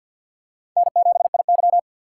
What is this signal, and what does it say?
Morse code, heard as a keyed tone, about 49 wpm, 700 Hz. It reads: N7IY
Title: USA Call Signs - 4 Characters - 2x Character Spacing - 50wpm